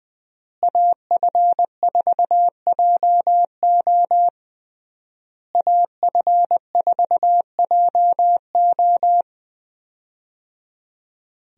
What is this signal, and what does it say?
Morse code, heard as a keyed tone, about 20 wpm, 700 Hz. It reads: AF4JO AF4JO
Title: USA Call Signs - 5 Characters - Repeated 3x - 20wpm